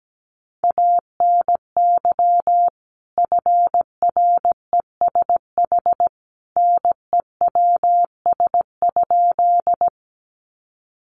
Morse code, heard as a keyed tone, about 17 wpm, 700 Hz. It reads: ANY FRESH NEWS?
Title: Sentences from Top 1200 Words 17wpm